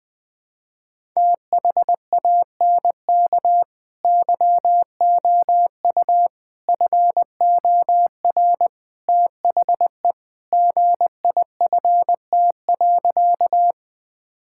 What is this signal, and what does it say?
Morse code, heard as a keyed tone, about 20 wpm, 700 Hz. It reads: THANK YOU FOR THE GIFT.
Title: Sentences from Top 1000 Words - Review Part 6 - 20wpm